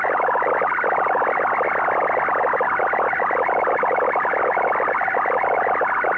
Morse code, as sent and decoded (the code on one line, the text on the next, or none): none